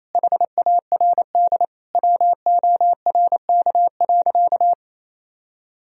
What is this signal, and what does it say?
Morse code, heard as a keyed tone, about 28 wpm, 700 Hz. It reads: HARD WORK.
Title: Sentences from Top 1000 Words - Review Part 4 - 28wpm